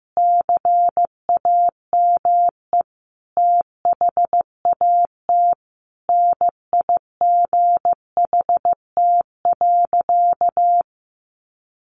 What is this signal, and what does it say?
Morse code, heard as a keyed tone, about 15 wpm, 700 Hz. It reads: CAME THAT NIGHT.
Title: Sentences from Top 1400 Words 15wpm